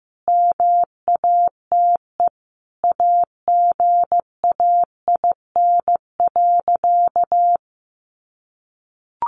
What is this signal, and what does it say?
Morse code, heard as a keyed tone, about 15 wpm, 700 Hz. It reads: MATE AGAIN.
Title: A Sound Of Thunder 15wpm